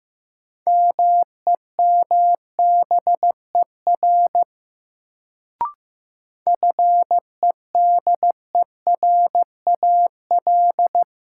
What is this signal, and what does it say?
Morse code, heard as a keyed tone, about 15 wpm, 700 Hz. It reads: MEMBER FEDERAL
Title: Sets of 3 Words - New Word Drill - 15wpm